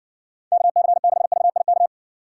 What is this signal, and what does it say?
Morse code, heard as a keyed tone, about 60 wpm, 700 Hz. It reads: KC6VIK